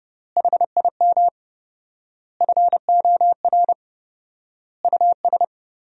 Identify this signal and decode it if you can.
Morse code, heard as a keyed tone, about 30 wpm, 700 Hz. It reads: HIM FOR US
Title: 4x Word-Spacing - Sentences from Top 500 Words - 30wpm